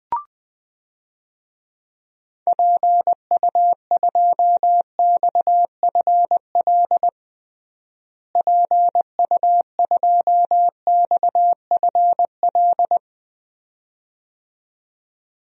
Morse code, heard as a keyed tone, about 20 wpm, 700 Hz. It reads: PU2XFL PU2XFL
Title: DXCC-100 Call Signs - 6 Characters - Repeated 2x - 20wpm